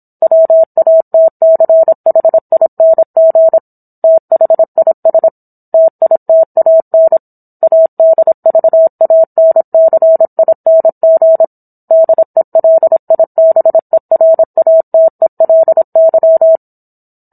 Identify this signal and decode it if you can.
Morse code, heard as a keyed tone, about 27 wpm, 650 Hz. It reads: WATCHING THIS TITAN ADVANCING DELIBERATELY